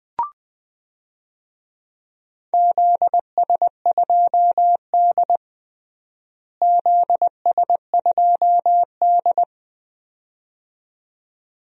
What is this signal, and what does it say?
Morse code, heard as a keyed tone, about 20 wpm, 700 Hz. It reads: ZS2D ZS2D
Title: DXCC-100 Call Signs - 4 Characters - Repeated 2x - 20wpm